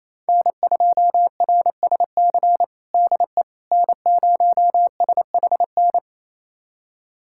Morse code, heard as a keyed tone, about 28 wpm, 700 Hz. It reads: N2RSC DE N0SHN
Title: USA Callsign DE Callsign - 5 Characters - 28wpm